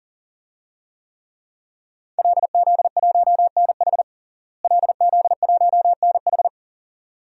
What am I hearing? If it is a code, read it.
Morse code, heard as a keyed tone, about 37 wpm, 700 Hz. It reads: LZ1NH LZ1NH
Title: DXCC-100 Call Signs - 5 Characters - Repeated 2x - 40wpm